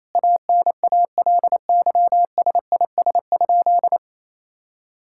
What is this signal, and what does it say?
Morse code, heard as a keyed tone, about 27 wpm, 700 Hz. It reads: ANALYSIS?
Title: Sentences from Top 1000 Words - Review Part 5 - 28wpm